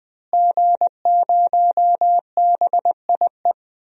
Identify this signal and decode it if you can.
Morse code, heard as a keyed tone, about 21 wpm, 700 Hz. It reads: G0BIE
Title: DXCC-100 Call Signs - 5 Characters - Repeated 3x - 20wpm